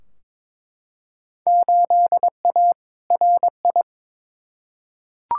Morse code, heard as a keyed tone, about 22 wpm, 700 Hz. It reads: 8A RI
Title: Field Day Exchanges Encore - 22wpm